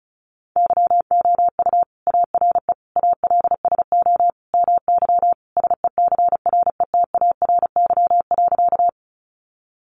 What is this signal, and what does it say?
Morse code, heard as a keyed tone, about 34 wpm, 700 Hz. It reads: YOU ARE ALSO MY SECRETARY.
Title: Sentences from Top 1300 Words 35wpm